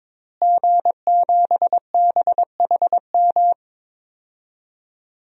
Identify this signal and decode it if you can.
Morse code, heard as a keyed tone, about 22 wpm, 700 Hz. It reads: G7BHM
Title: DXCC-100 Call Signs - 5 Characters - Repeated 3x - 22wpm